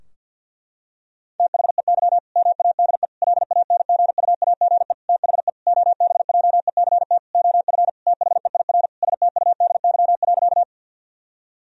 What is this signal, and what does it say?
Morse code, heard as a keyed tone, about 50 wpm, 700 Hz. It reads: THEY MADE LANGUAGE THE OBJECT OF THEIR STUDY.
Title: Sentences from Top 1000 Words - Review Part 3 - 50wpm